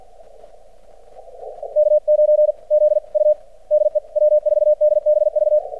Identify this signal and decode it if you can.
Morse code, heard as a keyed tone, about 46 wpm, spaced as for 40 wpm, 600 Hz. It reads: K9ZU DE W4DBV
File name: archived / CRSnet